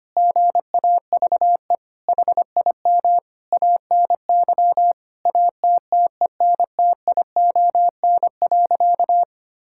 Morse code, heard as a keyed tone, about 25 wpm, 700 Hz. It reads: GAVE HIM ANY ATTENTION.
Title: Sentences from Top 1000 Words - Review Part 6 - 25wpm